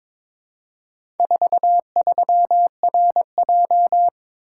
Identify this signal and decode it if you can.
Morse code, heard as a keyed tone, about 22 wpm, 700 Hz. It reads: 43RJ
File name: practice / 22wpm / Four Letter-Number Combinations - 22wpm